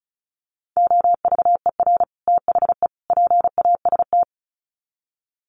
Morse code, heard as a keyed tone, about 34 wpm, 700 Hz. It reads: OVER THE PAST
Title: Sets of 3 Words 35wpm